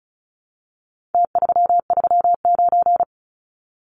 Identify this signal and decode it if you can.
Morse code, heard as a keyed tone, about 34 wpm, 700 Hz. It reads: T339
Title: Numbers - Contest Style - 4-Digit 35wpm